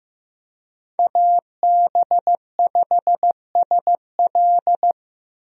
Morse code, heard as a keyed tone, about 15 wpm, 700 Hz. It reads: AB5SL
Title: USA Call Signs - 5 Characters - 15wpm